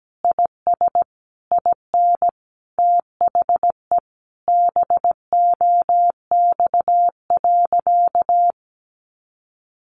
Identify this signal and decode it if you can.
Morse code, heard as a keyed tone, about 17 wpm, 700 Hz. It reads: IS IN THE BOX.
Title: Easy Phrases 17wpm